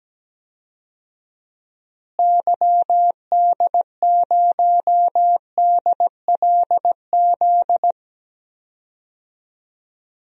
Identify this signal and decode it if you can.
Morse code, heard as a keyed tone, about 17 wpm, 700 Hz. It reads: YD0DLZ
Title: DXCC-100 Call Signs - 6 Characters - Repeated 3x - 17wpm